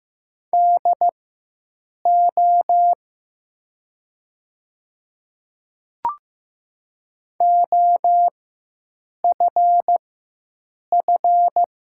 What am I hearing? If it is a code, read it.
Morse code, heard as a keyed tone, about 15 wpm, spaced as for 7 wpm, 700 Hz. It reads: DO OFF
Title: 4x Character Spacing - Top 200 Words - 15wpm